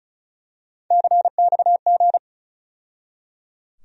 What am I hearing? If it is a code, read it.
Morse code, heard as a keyed tone, about 36 wpm, 700 Hz. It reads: CXG
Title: USA Call Sign Postfixes - 3 Letters 35wpm